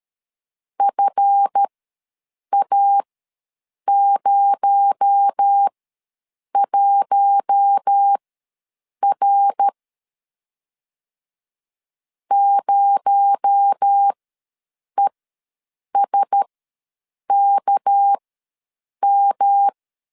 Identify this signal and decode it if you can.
Morse code, heard as a keyed tone, about 13 wpm, spaced as for 7 wpm, 800 Hz. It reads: FA01R 0ESKM
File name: LEZIONI / 7 GF9Z / 7 GF ws 12wpm 3140